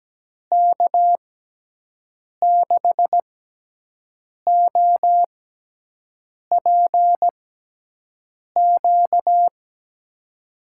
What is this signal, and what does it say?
Morse code, heard as a keyed tone, about 17 wpm, spaced as for 6 wpm, 700 Hz. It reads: K6OPQ
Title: USA Call Signs - 6x Character Spacing - 17wpm